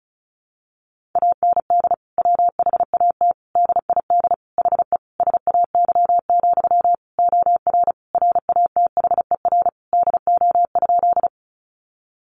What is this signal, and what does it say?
Morse code, heard as a keyed tone, about 34 wpm, 700 Hz. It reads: AND WHAT DID HE SAY, OR RATHER DO?
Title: Sentences from Top 700 Words 35wpm